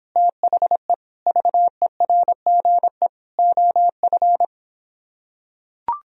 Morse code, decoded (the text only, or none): THE VERGE OF